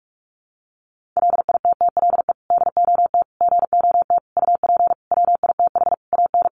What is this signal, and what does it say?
Morse code, heard as a keyed tone, about 44 wpm, 700 Hz. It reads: LITTLE DOT GOT UP WITH AN
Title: Sentences from Top 1500 Words 45wpm